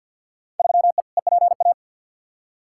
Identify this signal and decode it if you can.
Morse code, heard as a keyed tone, about 53 wpm, 700 Hz. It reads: 3E EPA
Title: Field Day Exchanges Encore - 50wpm